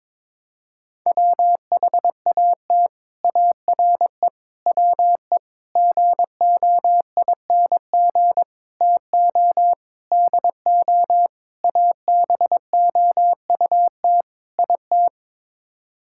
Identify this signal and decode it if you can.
Morse code, heard as a keyed tone, about 22 wpm, 700 Hz. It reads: WHAT ARE WE GOING TO DO ABOUT IT
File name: practice / 22wpm / Sets of 8 Words - 22wpm